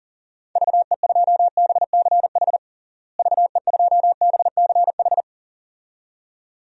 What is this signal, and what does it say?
Morse code, heard as a keyed tone, about 40 wpm, 700 Hz. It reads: VE2BCH VE2BCH
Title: DXCC-100 Call Signs - 6 Characters - Repeated 2x - 40wpm